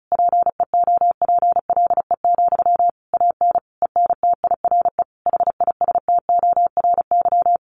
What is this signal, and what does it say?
Morse code, heard as a keyed tone, about 34 wpm, 700 Hz. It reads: PEOPLE, AN ENTIRE HISTORY
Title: A Sound Of Thunder 35wpm